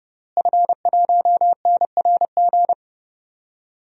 Morse code, heard as a keyed tone, about 30 wpm, 700 Hz. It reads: F1NRG
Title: DXCC-100 Call Signs - 5 Characters - Repeated 3x - 30wpm